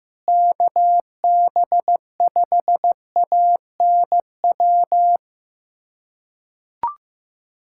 Morse code, heard as a keyed tone, about 15 wpm, 700 Hz. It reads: KB5ANW
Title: USA Callsign DE Callsign - 6 Characters - 15wpm